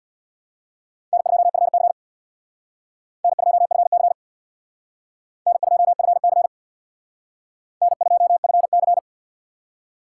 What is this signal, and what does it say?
Morse code, heard as a keyed tone, about 61 wpm, 700 Hz. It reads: N2FC N2FC N2FC N2FC
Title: Speed Racing - USA 1x2 Call Signs 50wpm